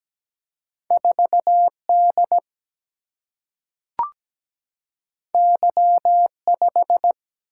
Morse code, heard as a keyed tone, about 17 wpm, 700 Hz. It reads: 4D Y5
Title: Single-Letter Single-Number 17wpm